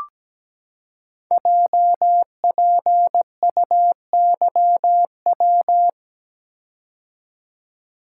Morse code, heard as a keyed tone, about 17 wpm, 700 Hz. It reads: JPUYW
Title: Code Groups - 5 Letters - 17wpm